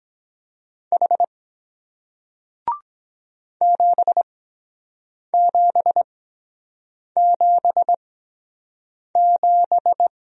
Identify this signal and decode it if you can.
Morse code, heard as a keyed tone, about 23 wpm, spaced as for 7 wpm, 700 Hz. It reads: H 7777